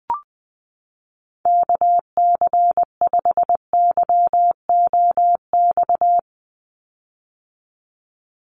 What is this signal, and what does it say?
Morse code, heard as a keyed tone, about 20 wpm, 700 Hz. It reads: KC5YOX
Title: USA Call Signs 20wpm